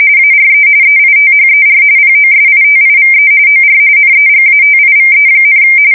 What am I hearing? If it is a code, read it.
RTTY, about 45.45 baud. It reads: ARE EITHER POSITIVE OR NEGATIVE,
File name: fsk170 / printing-telegraph